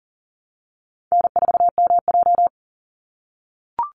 Morse code, heard as a keyed tone, about 39 wpm, 700 Hz. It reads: N4MJ
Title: NA SOTA Chaser Callsigns 40wpm